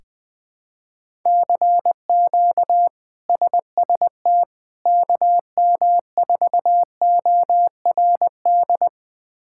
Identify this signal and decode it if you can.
Morse code, heard as a keyed tone, about 21 wpm, 700 Hz. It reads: CQ SST KM4ORD
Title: K1USN SST QSO Practice - 20wpm